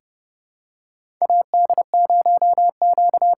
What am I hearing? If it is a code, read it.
Morse code, heard as a keyed tone, about 32 wpm, 700 Hz. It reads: AD0Q
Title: USA Call Signs - 5 Characters - 8x Character Spacing - 30wpm